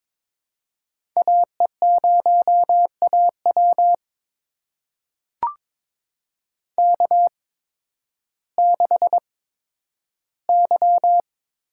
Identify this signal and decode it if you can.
Morse code, heard as a keyed tone, about 22 wpm, spaced as for 12 wpm, 700 Hz. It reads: AE0AW K6Y
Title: USA Call Signs - 8x Character Spacing - 22wpm